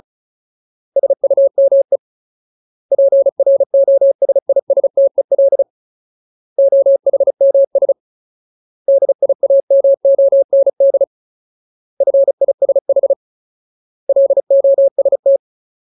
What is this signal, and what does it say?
Morse code, heard as a keyed tone, about 34 wpm, 550 Hz. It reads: SUME PROSISTEL OHMS DIAMOND FISH LOST
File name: Ultimate Word List / 35WPM Ultimate